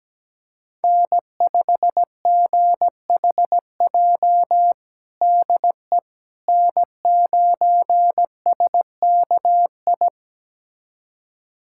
Morse code, heard as a keyed tone, about 17 wpm, 700 Hz. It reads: N5GHJ DE N9SKI
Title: USA Callsign DE Callsign - 5 Characters - 17wpm